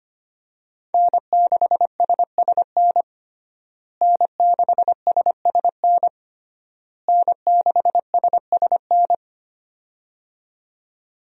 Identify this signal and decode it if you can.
Morse code, heard as a keyed tone, about 25 wpm, 700 Hz. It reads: N6SSN N6SSN N6SSN
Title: USA Call Signs - 5 Characters - Repeated 3x - 25wpm